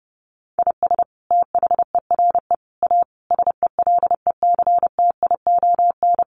Morse code, heard as a keyed tone, about 28 wpm, 700 Hz. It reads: IS THERE A SELECTION
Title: Sentences from Top 1400 Words 30wpm